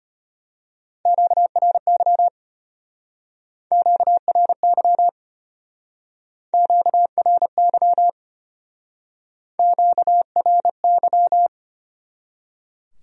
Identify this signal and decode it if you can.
Morse code, heard as a keyed tone, about 33 wpm, 700 Hz. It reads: QRY QRY QRY QRY